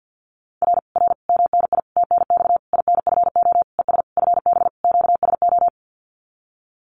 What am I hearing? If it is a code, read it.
Morse code, heard as a keyed tone, about 50 wpm, 700 Hz. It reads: R R MNI TNX INFO ES FB QSO
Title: Common QSO Phrases Encore 50wpm